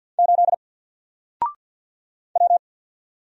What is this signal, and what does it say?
Morse code, heard as a keyed tone, about 50 wpm, 700 Hz. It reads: 8 W
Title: Single Letter-Number 50wpm